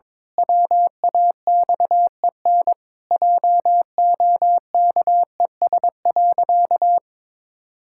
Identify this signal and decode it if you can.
Morse code, heard as a keyed tone, about 22 wpm, 700 Hz. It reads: WAXEN JOKES.
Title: Pangram Challenge v2 - 22wpm